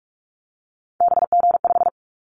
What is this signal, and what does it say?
Morse code, heard as a keyed tone, about 46 wpm, 700 Hz. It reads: BG5
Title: Callsign Prefixes for Top 100 DXCC - 45wpm